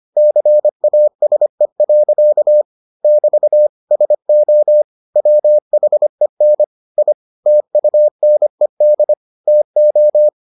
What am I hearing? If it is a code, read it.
Morse code, heard as a keyed tone, about 25 wpm, 600 Hz. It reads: CASE. <BT> SO WHEN I TUNED TO